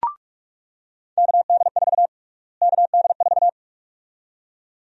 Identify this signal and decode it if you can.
Morse code, heard as a keyed tone, about 46 wpm, 700 Hz. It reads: KD4 KD4
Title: USA Call Sign Prefixes - Repeated 2x - 45wpm